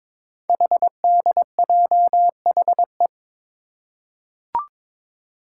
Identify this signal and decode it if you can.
Morse code, heard as a keyed tone, about 21 wpm, 700 Hz. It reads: HDJHE
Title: Code Groups - 5 Letters - 22wpm